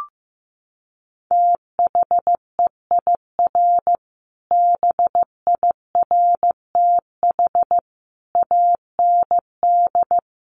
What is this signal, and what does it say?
Morse code, heard as a keyed tone, about 15 wpm, 700 Hz. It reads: THEIR BIRTH AND
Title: Sentences from Top 900 Words 15wpm